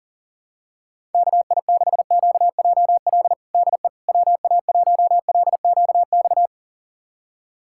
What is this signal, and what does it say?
Morse code, heard as a keyed tone, about 40 wpm, 690 Hz. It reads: KI6QJL DE WA1LQX
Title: USA Callsign DE Callsign - 6 Characters - 40wpm